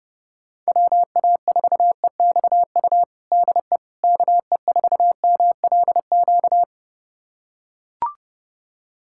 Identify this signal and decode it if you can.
Morse code, heard as a keyed tone, about 30 wpm, 700 Hz. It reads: WA4EXU DE KE4MLQ
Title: USA Callsign DE Callsign - 6 Characters - 30wpm